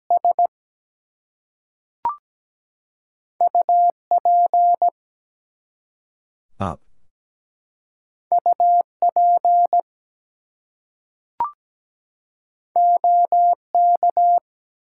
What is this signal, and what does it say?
Morse code, heard as a keyed tone, about 17 wpm, 700 Hz. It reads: S UP UP OK
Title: Two Letter Words - 17wpm